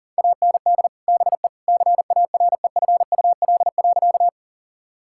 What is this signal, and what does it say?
Morse code, heard as a keyed tone, about 40 wpm, 700 Hz. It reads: AND BE CAREFUL.
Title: Sentences from Top 1600 Words 40wpm